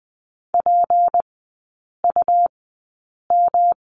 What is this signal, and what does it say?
Morse code, heard as a keyed tone, about 20 wpm, spaced as for 8 wpm, 700 Hz. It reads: PUM
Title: 191 Lesson 31e - Letter v - Three Characters 20wpm